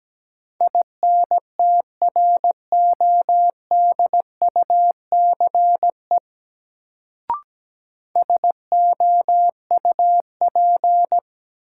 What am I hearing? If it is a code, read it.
Morse code, heard as a keyed tone, about 17 wpm, 700 Hz. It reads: INTRODUCE SOUP
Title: Sentences from Top 1600 Words - New Word Drill Alt - 17wpm